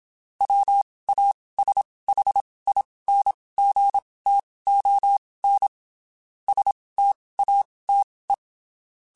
Morse code, heard as a keyed tone, about 27 wpm, spaced as for 19 wpm, 800 Hz. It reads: WASHINGTON STATE
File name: (spaced) / QSOs Spaced-mp3s / WD0BGP 1 @25 sp